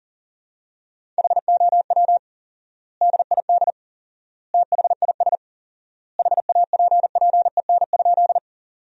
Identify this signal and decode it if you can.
Morse code, heard as a keyed tone, about 40 wpm, 700 Hz. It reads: HOW DID THIS HAPPEN?
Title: 4x Word-Spacing - Sentences from Top 400 Words - 40wpm